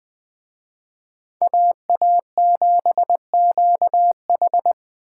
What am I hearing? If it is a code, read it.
Morse code, heard as a keyed tone, about 24 wpm, 700 Hz. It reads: AA7QH
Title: Speed Racing - USA 2x2 Call Signs 20wpm